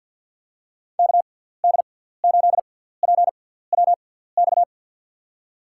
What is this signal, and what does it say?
Morse code, heard as a keyed tone, about 52 wpm, spaced as for 17 wpm, 700 Hz. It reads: KD8PWX
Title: USA Call Signs - 6 Characters - 6x Character Spacing - 50wpm